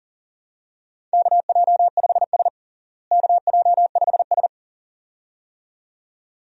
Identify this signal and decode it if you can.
Morse code, heard as a keyed tone, about 40 wpm, 700 Hz. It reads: KJ5S KJ5S
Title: USA Call Signs - 4 Characters - Repeated 2x - 40wpm